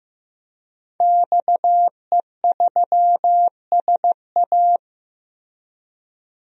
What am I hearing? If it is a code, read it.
Morse code, heard as a keyed tone, about 15 wpm, 700 Hz. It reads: XE3SA
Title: DXCC-100 Call Signs - 5 Characters - Repeated 2x - 15wpm